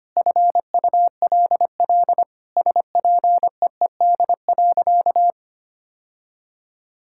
Morse code, heard as a keyed tone, about 25 wpm, 700 Hz. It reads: FULL SPEED.
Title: Sentences from Top 1000 Words - Review Part 5 - 25wpm